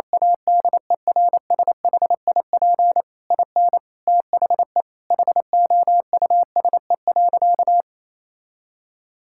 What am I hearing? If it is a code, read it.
Morse code, heard as a keyed tone, about 28 wpm, 700 Hz. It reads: ADERSHIP IN THE HOUSE.
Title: Sentences from Top 1600 Words 28wpm